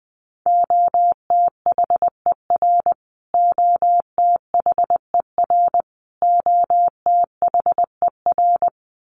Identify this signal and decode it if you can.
Morse code, heard as a keyed tone, about 20 wpm, 700 Hz. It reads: OTHER OTHER OTHER
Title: Top 500 Words Repeated 20wpm